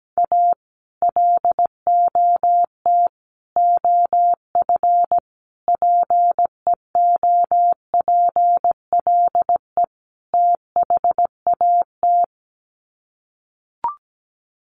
Morse code, heard as a keyed tone, about 17 wpm, 700 Hz. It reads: A LOT OF PEOPLE THAT
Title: Sets of 5 Words Encore2 17wpm